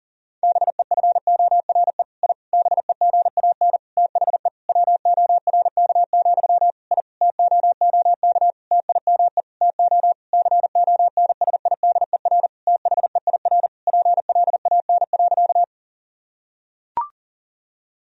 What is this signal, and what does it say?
Morse code, heard as a keyed tone, about 40 wpm, 700 Hz. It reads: BEFORE I BEGAN THE WORK, I TOOK TIME TO CONSIDER THEIR PLAN.
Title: Sentences from Top 1000 Words - Review Part 4 - 40wpm